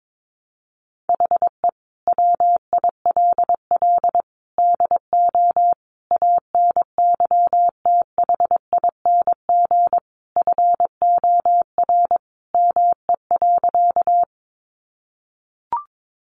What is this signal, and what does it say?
Morse code, heard as a keyed tone, about 22 wpm, 700 Hz. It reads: HE WILL DO ANYTHING FOR ME.
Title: Sentences from Top 1000 Words 22wpm